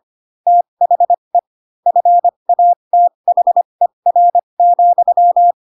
Morse code, heard as a keyed tone, about 25 wpm, spaced as for 22 wpm, 700 Hz. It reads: THE FATHER,
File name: Pinocchio / Pinocchio ch3 18 25 ik1hge